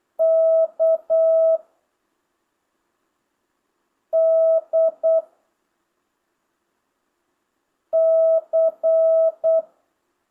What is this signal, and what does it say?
Morse code, heard as a keyed tone, about 8 wpm, 660 Hz. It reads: K D C
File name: jh8cbh / 2025cw